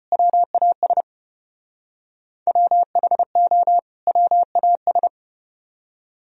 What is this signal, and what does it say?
Morse code, heard as a keyed tone, about 34 wpm, 700 Hz. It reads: WAS WHO WAS